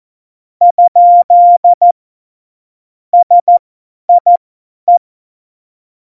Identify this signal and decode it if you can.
Morse code, heard as a keyed tone, about 14 wpm, spaced as for 10 wpm, 700 Hz. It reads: ? SIE
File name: Morsekurs / Witze / Witz 17.Dezember 2019, 50 F70